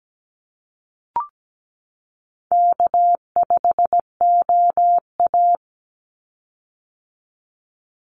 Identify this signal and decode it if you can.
Morse code, heard as a keyed tone, about 17 wpm, 700 Hz. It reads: K5OA
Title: Easy Call Signs - 17wpm